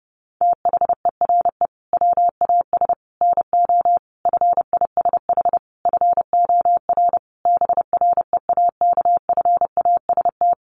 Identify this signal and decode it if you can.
Morse code, heard as a keyed tone, about 30 wpm, 700 Hz. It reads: THERE WAS NO FISH FOR BREAKFAST
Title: Sentences from Top 1500 Words 30wpm